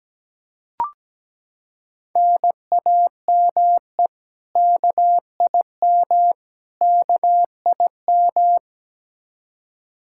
Morse code, heard as a keyed tone, about 17 wpm, 700 Hz. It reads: NAME KIM KIM
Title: K1USN SST QSO Practice 17wpm